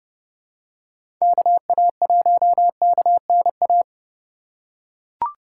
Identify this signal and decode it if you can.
Morse code, heard as a keyed tone, about 30 wpm, 700 Hz. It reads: KA1KNA
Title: USA Call Signs - 6 Characters - 2x Character Spacing - 30wpm